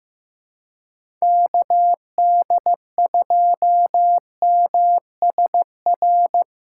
Morse code, heard as a keyed tone, about 15 wpm, 700 Hz. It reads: KD2MSR
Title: USA Callsign DE Callsign 15wpm